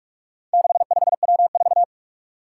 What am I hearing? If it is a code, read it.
Morse code, heard as a keyed tone, about 44 wpm, 700 Hz. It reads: 65W4
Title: Four Challenging Letter-Number Combinations 45wpm